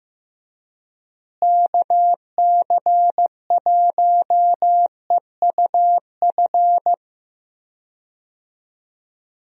Morse code, heard as a keyed tone, about 15 wpm, 700 Hz. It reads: KC1EUF
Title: USA Call Signs - 6 Characters - Repeated 2x - 15wpm